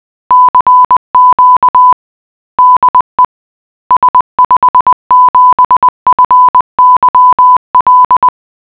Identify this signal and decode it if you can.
Morse code, heard as a keyed tone, about 20 wpm, 1000 Hz. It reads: CQ DE S57FYL